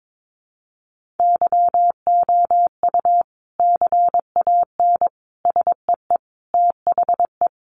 Morse code, heard as a keyed tone, about 22 wpm, 700 Hz. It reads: YOU CAN SEE THE
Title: Sentences from Top 1500 Words 22wpm